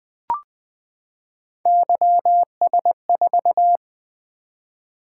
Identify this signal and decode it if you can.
Morse code, heard as a keyed tone, about 20 wpm, 700 Hz. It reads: YS4
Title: Three Letter-Number Combinations - 20wpm